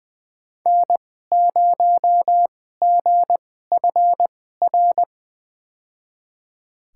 Morse code, heard as a keyed tone, about 20 wpm, spaced as for 14 wpm, 700 Hz. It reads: N0GFR
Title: USA Call Signs - 5 Characters - 2x Character Spacing - 20wpm